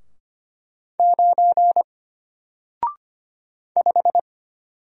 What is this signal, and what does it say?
Morse code, heard as a keyed tone, about 25 wpm, 700 Hz. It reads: 9 5
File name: Single-Letter-Number-25wpm